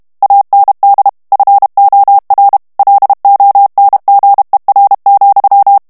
Morse code, heard as a keyed tone, about 32 wpm, 800 Hz. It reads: AND FOR LONGER,
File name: morse mp3 / ANGLAIS texte clair 1 vitesse 30 mots minute